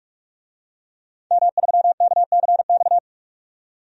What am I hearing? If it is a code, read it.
Morse code, heard as a keyed tone, about 42 wpm, 700 Hz. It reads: M3KCX
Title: DXCC-100 Call Signs - 5 Characters - 45wpm